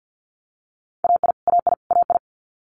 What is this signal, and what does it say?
Morse code, heard as a keyed tone, about 50 wpm, 690 Hz. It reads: AI AI AI
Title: Multi-letter Phonemes Repeated 3x 50wpm